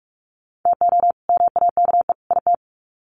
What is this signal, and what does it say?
Morse code, heard as a keyed tone, about 44 wpm, 700 Hz. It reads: TO MAKE IT
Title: Sets of 3 Words 45wpm